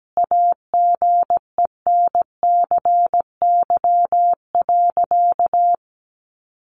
Morse code, heard as a keyed tone, about 17 wpm, 700 Hz. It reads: AGENCY.
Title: Sentences from Top 1300 Words 17wpm